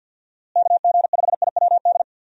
Morse code, heard as a keyed tone, about 50 wpm, 700 Hz. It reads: KG5IWD
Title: USA Call Signs - 6 Characters - Repeated 2x - 50wpm